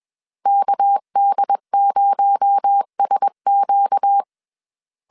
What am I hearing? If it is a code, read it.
Morse code, heard as a keyed tone, about 21 wpm, 800 Hz. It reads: KD0SQ
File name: CW / DET / 1 6 1500C S B 20wpm 2048